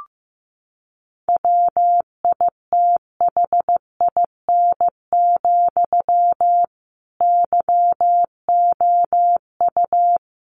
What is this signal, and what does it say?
Morse code, heard as a keyed tone, about 15 wpm, 700 Hz. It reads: WITHIN, YOU
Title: A Sound Of Thunder 15wpm